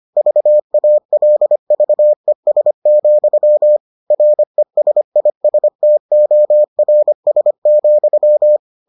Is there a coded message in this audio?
Yes: Morse code from 0.2 to 8.6 s, heard as a keyed tone, about 25 wpm, 600 Hz. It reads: VALVES, RESISTORS,